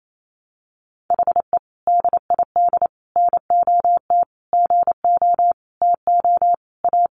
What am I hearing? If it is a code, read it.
Morse code, heard as a keyed tone, about 28 wpm, 700 Hz. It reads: HE DID NOT GO TO A
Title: Sentences from Top 1200 Words 28wpm